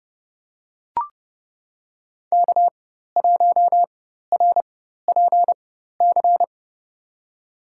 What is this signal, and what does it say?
Morse code, heard as a keyed tone, about 30 wpm, spaced as for 14 wpm, 700 Hz. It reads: K1RPC
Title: USA Call Signs - 4x Character Spacing - 30wpm